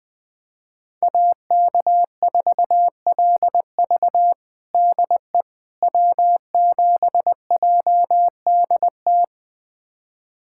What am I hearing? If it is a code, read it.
Morse code, heard as a keyed tone, about 20 wpm, 700 Hz. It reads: AK4LV DE W7JDT
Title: USA Callsign DE Callsign - 5 Characters - 20wpm